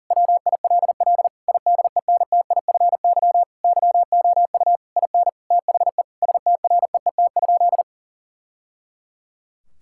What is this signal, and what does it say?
Morse code, heard as a keyed tone, about 37 wpm, 700 Hz. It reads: WILL IDENTIFY YOU IN THE STREET?
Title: Sentences from Top 1000 Words - Review Part 6 - 40wpm